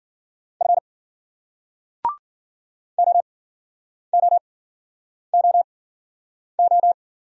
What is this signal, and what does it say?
Morse code, heard as a keyed tone, about 50 wpm, spaced as for 9 wpm, 700 Hz. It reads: F OOOO
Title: Speed-Racing Single Letters 40wpm